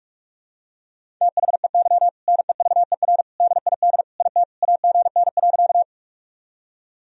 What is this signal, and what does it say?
Morse code, heard as a keyed tone, about 46 wpm, 700 Hz. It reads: THEY NEVER DID IT AGN.
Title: Sentences from Top 300 Words and Abbreviations 45wpm